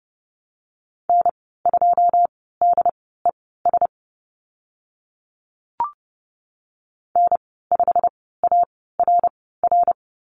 Morse code, heard as a keyed tone, about 32 wpm, spaced as for 17 wpm, 700 Hz. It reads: N2DES N5ARR
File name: Lesson 15g - Number 5 - Call Signs 30wpm